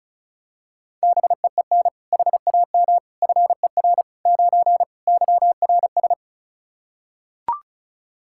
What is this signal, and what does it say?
Morse code, heard as a keyed tone, about 36 wpm, 700 Hz. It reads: BEEN HAM FER 9 YRS
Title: Practice QSO - Exchange 3 - More Info - 35wpm